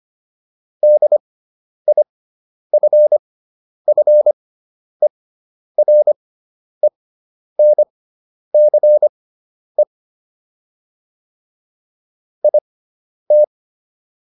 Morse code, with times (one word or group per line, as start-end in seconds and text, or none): 0.8-9.8 DIFFERENCE
12.4-13.4 IT